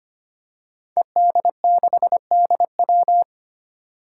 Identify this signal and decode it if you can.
Morse code, heard as a keyed tone, about 25 wpm, 700 Hz. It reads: ED6DW
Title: Five Challenging Letter-Number Combinations 25wpm